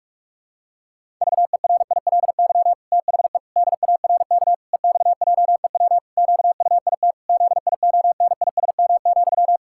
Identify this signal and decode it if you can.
Morse code, heard as a keyed tone, about 41 wpm, 700 Hz. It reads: VERILY THE DARK EXJEW QUIT ZIONISM,
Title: Pangram Challenge v2 - 45wpm